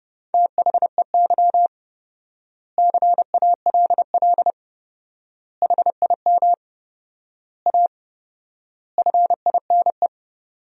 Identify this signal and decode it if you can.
Morse code, heard as a keyed tone, about 32 wpm, 700 Hz. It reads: THEY CALL HIM A FINE
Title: 4x Word-Spacing - Sentences from Top 500 Words - 30wpm